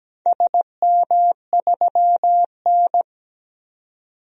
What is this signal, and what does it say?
Morse code, heard as a keyed tone, about 17 wpm, 700 Hz. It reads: SM3N
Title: DXCC-100 Call Signs - 4 Characters - 17wpm